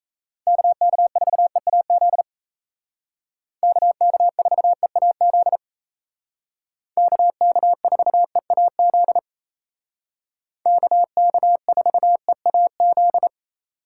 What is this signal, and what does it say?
Morse code, heard as a keyed tone, about 35 wpm, 700 Hz. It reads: KK4EAZ KK4EAZ KK4EAZ KK4EAZ
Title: Speed Racing - USA Call Signs 28wpm